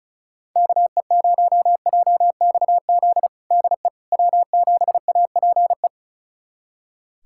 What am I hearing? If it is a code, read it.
Morse code, heard as a keyed tone, about 36 wpm, 700 Hz. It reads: KE0JXZ DE W7APE
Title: USA Callsign DE Callsign 35wpm